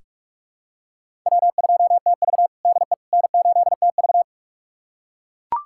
Morse code, heard as a keyed tone, about 44 wpm, 700 Hz. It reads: W2TV DE N8TV
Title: USA Callsign DE Callsign - 4 Characters - 45wpm